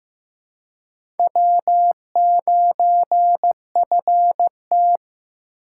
Morse code, heard as a keyed tone, about 15 wpm, 700 Hz. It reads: W9FT
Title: USA Call Signs - 4 Characters - 8x Character Spacing - 15wpm